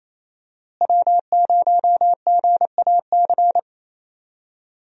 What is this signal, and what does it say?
Morse code, heard as a keyed tone, about 28 wpm, 700 Hz. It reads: W0GAC
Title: USA Call Signs - 5 Characters - Repeated 2x - 28wpm